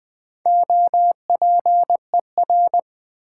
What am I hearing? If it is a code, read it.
Morse code, heard as a keyed tone, about 20 wpm, 700 Hz. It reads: OPER
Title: Build A Word - 4 Letters At A Time - 11-Letter Words - 20wpm